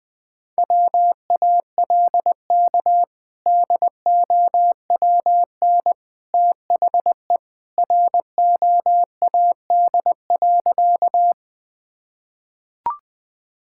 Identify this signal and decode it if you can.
Morse code, heard as a keyed tone, about 21 wpm, 700 Hz. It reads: WALK DOWN THE ROAD.
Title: Sentences with 2-Word Phrases - Spoken Beginning - 20wpm